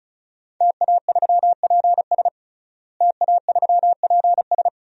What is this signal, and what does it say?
Morse code, heard as a keyed tone, about 34 wpm, 700 Hz. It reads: TA3PS TA3PS
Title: DXCC-100 Call Signs - 5 Characters - Repeated 3x - 35wpm